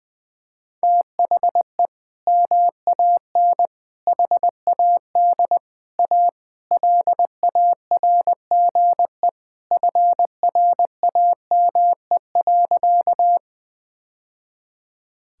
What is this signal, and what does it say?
Morse code, heard as a keyed tone, about 20 wpm, 700 Hz. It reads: THE MAN HAD A LARGE FRAME.
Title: Sentences from Top 1000 Words - Review Part 6 - 20wpm